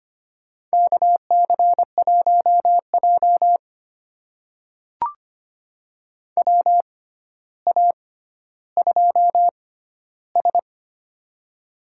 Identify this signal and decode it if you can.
Morse code, heard as a keyed tone, about 25 wpm, spaced as for 13 wpm, 700 Hz. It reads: KC1J WA2S
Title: USA Call Signs - 4 Characters - 6x Character Spacing - 25wpm